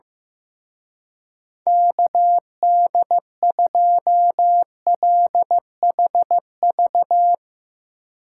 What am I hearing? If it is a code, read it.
Morse code, heard as a keyed tone, about 15 wpm, 700 Hz. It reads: KD2LHV
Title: Speed Racing - USA 2x3 Call Signs 15wpm